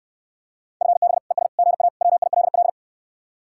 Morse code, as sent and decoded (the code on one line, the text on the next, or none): ..-. -... / . ... / --. .- / .-- .. .-.. .-..
FB ES GA WILL